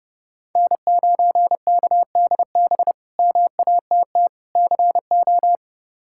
Morse code, heard as a keyed tone, about 30 wpm, 700 Hz. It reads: N9KDB MATT CO
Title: K1USN SST QSO Practice - 30wpm